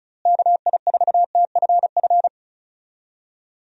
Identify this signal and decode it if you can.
Morse code, heard as a keyed tone, about 34 wpm, 700 Hz. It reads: KI4TFF
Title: USA Call Signs - 6 Characters - Repeated 3x - 35wpm